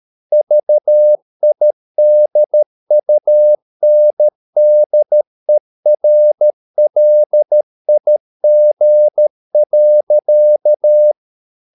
Morse code, heard as a keyed tone, about 13 wpm, 600 Hz. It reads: VIDUNDERLIG.
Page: Ferie 13 wpm